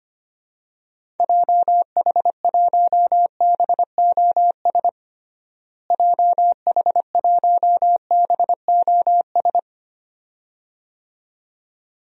Morse code, heard as a keyed tone, about 25 wpm, 700 Hz. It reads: JH1BOS JH1BOS
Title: DXCC-100 Call Signs - 6 Characters - Repeated 2x - 25wpm